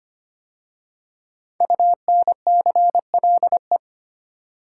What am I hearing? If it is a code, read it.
Morse code, heard as a keyed tone, about 25 wpm, 700 Hz. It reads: UNCLE
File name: 5 Letter Words - First 3 Letters Given - 25wpm